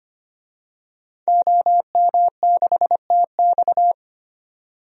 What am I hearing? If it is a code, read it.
Morse code, heard as a keyed tone, about 25 wpm, 700 Hz. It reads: OM6TX
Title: DXCC-100 Call Signs - 5 Characters - 25wpm